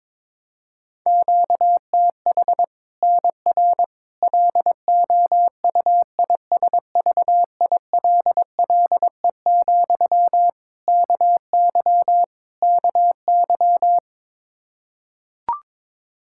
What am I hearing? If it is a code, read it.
Morse code, heard as a keyed tone, about 22 wpm, 700 Hz. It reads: QTH NR LOUISVILLE, KY KY
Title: K1USN SST QSO Practice 22wpm